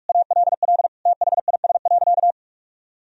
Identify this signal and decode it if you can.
Morse code, heard as a keyed tone, about 42 wpm, 700 Hz. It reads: ALL THIS.
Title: Sentences from Top 1000 Words - Review Part 1 - 45wpm